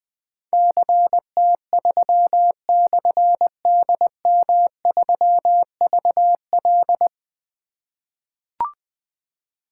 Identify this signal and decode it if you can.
Morse code, heard as a keyed tone, about 20 wpm, 700 Hz. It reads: CT3/DM3VL
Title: Difficult Call Signs 20wpm